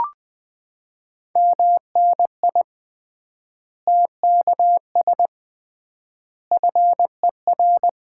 Morse code, heard as a keyed tone, about 20 wpm, 700 Hz. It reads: MNI TKS FER
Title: Common QSO Phrases - 3x Word-Spacing - 20wpm